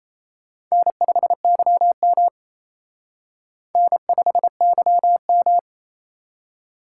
Morse code, heard as a keyed tone, about 34 wpm, 700 Hz. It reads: N5YM N5YM
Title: Speed Racing - USA 1x2 Call Signs 28wpm